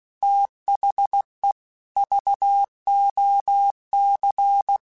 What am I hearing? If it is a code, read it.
Morse code, heard as a keyed tone, about 16 wpm, 800 Hz. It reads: THE VOC
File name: QSOs / QSOs-mp3s / WD0BGP 1@15